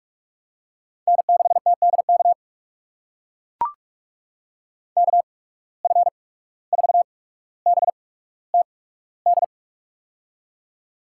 Morse code, heard as a keyed tone, about 44 wpm, spaced as for 18 wpm, 690 Hz. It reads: N6TDK KF4BTD
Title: USA Call Signs - 8x Character Spacing - 45wpm